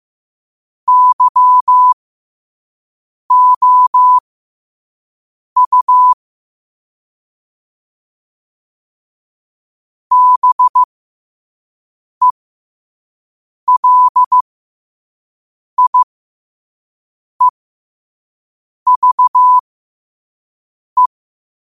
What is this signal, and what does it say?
Morse code, heard as a keyed tone, about 15 wpm, spaced as for 5 wpm, 1000 Hz. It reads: YOU BELIEVE
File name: qotdfiles / 5wpm